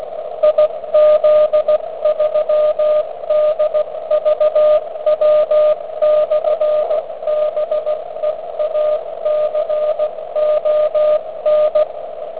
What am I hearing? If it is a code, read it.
Morse code, heard as a keyed tone, about 16 wpm, 610 Hz, with radio noise. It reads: IZ3DVW/BEACON